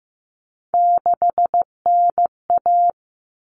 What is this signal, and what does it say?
Morse code, heard as a keyed tone, about 15 wpm, 700 Hz. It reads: 6NA
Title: Common Cut Numbers - 3-Digit 15wpm